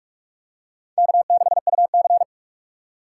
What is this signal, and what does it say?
Morse code, heard as a keyed tone, about 43 wpm, 700 Hz. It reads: K6UC
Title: USA Call Signs - 3x Character Spacing - 45wpm